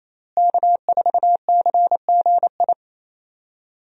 Morse code, heard as a keyed tone, about 28 wpm, 700 Hz. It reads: K4CGI
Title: USA Call Signs - 5 Characters - Repeated 3x - 28wpm